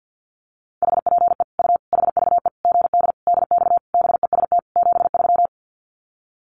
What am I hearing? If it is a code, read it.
Morse code, heard as a keyed tone, about 50 wpm, 700 Hz. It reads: HPE U HVE GD DX BEST 73
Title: Common QSO Phrases 50wpm